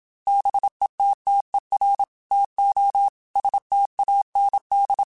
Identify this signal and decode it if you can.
Morse code, heard as a keyed tone, about 26 wpm, 800 Hz. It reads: BETTER TO STAND
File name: Quotes / Quotes-mp3s / StandFor @25